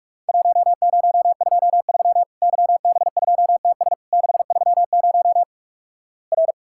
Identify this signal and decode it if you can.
Morse code, heard as a keyed tone, about 48 wpm, 700 Hz. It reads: 1023 YB2TS 630 R
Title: Worked All Europe DX Contest - QTC Exchange - 45wpm